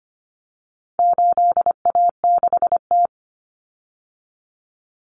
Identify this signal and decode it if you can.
Morse code, heard as a keyed tone, about 25 wpm, 700 Hz. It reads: 8A6T
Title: Common Cut Numbers - 4-Digit 25wpm